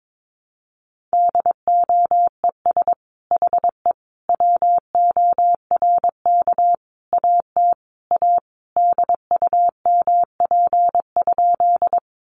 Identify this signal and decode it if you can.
Morse code, heard as a keyed tone, about 22 wpm, 700 Hz. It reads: DOES HE WORK AT A DUMP?
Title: Sentences from Top 1300 Words 22wpm